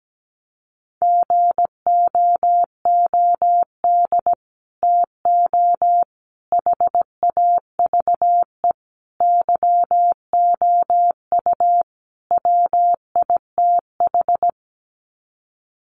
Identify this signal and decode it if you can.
Morse code, heard as a keyed tone, about 17 wpm, 700 Hz. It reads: GOOD TO HAVE YOU WITH
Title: Sets of 5 Words Encore2 17wpm